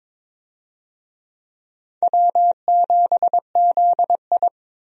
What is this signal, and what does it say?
Morse code, heard as a keyed tone, about 22 wpm, 700 Hz. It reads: W7ZI
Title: USA Call Signs - 4 Characters - 22wpm